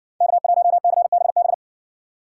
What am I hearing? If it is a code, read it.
Morse code, heard as a keyed tone, about 65 wpm, 700 Hz. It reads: K1CBB